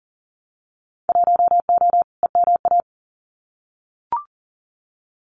Morse code, heard as a keyed tone, about 42 wpm, 700 Hz. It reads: 1O EMA
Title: Winter Field Day Exchanges 40wpm